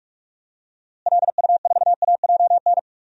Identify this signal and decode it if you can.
Morse code, heard as a keyed tone, about 44 wpm, 690 Hz. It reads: LU4AJN
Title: DXCC-100 Call Signs - 6 Characters - Repeated 2x - 45wpm